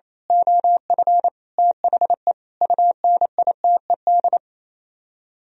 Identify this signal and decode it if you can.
Morse code, heard as a keyed tone, about 26 wpm, 700 Hz. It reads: OF THE UNITED
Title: Speed-Racing - Sets of 5 Words 28wpm